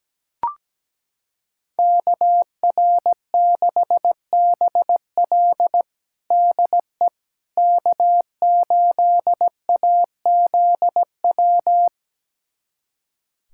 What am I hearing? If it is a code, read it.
Morse code, heard as a keyed tone, about 17 wpm, 700 Hz. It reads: KR6BL DE K8AZW
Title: USA Callsign DE Callsign - 5 Characters - 17wpm